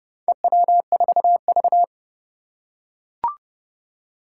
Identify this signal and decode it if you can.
Morse code, heard as a keyed tone, about 30 wpm, 700 Hz. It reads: EW4V
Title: Four Challenging Letter-Number Combinations 30wpm